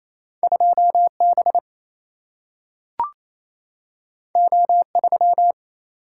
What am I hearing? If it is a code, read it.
Morse code, heard as a keyed tone, about 28 wpm, 700 Hz. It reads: 2B O3
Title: Single-Letter Single-Number 28wpm